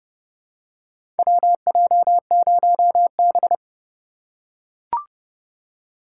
Morse code, heard as a keyed tone, about 30 wpm, 700 Hz. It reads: WJ0B
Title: USA Call Signs - 4 Characters - 4x Character Spacing - 30wpm